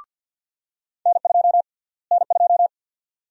Morse code, heard as a keyed tone, about 53 wpm, 700 Hz. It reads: N2 N2
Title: USA Call Sign Prefixes - Repeated 2x - 50wpm